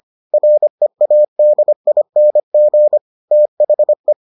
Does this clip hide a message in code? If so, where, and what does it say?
Yes: Morse code from 0.3 to 4.1 s, heard as a keyed tone, about 25 wpm, 600 Hz. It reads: READING THE